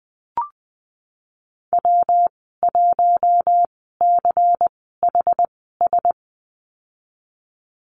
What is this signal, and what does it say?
Morse code, heard as a keyed tone, about 20 wpm, spaced as for 15 wpm, 700 Hz. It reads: W1CHS
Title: 122 Lesson 20g - Number 3 - Call Signs 20wpm